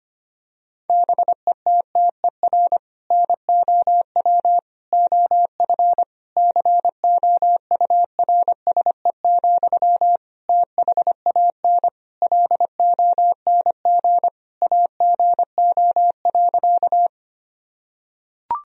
Morse code, heard as a keyed tone, about 25 wpm, 700 Hz. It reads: BETTER NOW OF COURSE, THAN LONG AGO.
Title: Sentences from Top 1000 Words - Review Part 3 - 25wpm